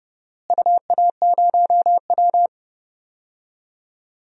Morse code, heard as a keyed tone, about 30 wpm, 700 Hz. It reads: UA0W
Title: DXCC-100 Call Signs - 4 Characters - Repeated 3x - 30wpm